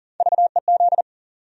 Morse code, heard as a keyed tone, about 40 wpm, 700 Hz. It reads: VEZ
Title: USA Call Sign Postfixes - 3 Letters 40wpm